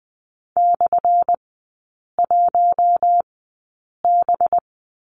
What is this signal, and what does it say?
Morse code, heard as a keyed tone, about 20 wpm, spaced as for 8 wpm, 700 Hz. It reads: /1B
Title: 184 Lesson 30e - Letter b - Three Characters 20wpm